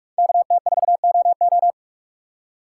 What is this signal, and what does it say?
Morse code, heard as a keyed tone, about 44 wpm, 700 Hz. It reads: KT4OO
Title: USA Call Signs - 5 Characters - 3x Character Spacing - 45wpm